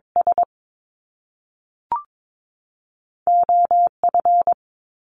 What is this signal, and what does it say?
Morse code, heard as a keyed tone, about 22 wpm, spaced as for 3 wpm, 700 Hz. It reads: SO F
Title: Top Two Letter Combinations in Words 22wpm